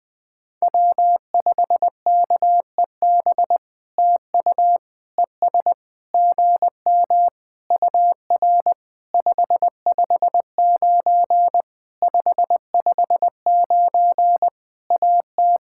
Morse code, heard as a keyed tone, about 20 wpm, 700 Hz. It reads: W5KEB TU ES GM UR 559 559 AT
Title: POTA QSO Practice - 20wpm